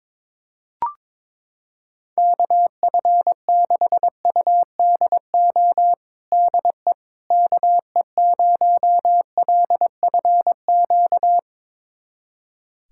Morse code, heard as a keyed tone, about 22 wpm, 700 Hz. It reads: KF6UDO DE KE0LFQ
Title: USA Callsign DE Callsign 22wpm